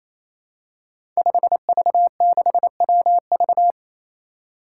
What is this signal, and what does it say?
Morse code, heard as a keyed tone, about 28 wpm, 700 Hz. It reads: 5V6WV
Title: Five Challenging Letter-Number Combinations 28wpm